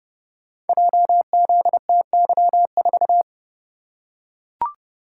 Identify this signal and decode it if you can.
Morse code, heard as a keyed tone, about 31 wpm, 700 Hz. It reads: JZTY4
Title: Code Groups - 5 Letter:Numbers - 30wpm